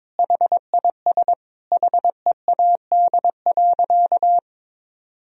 Morse code, heard as a keyed tone, about 21 wpm, 700 Hz. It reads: HIS HEAD.
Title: Sentences from Top 1000 Words - Review Part 3 - 22wpm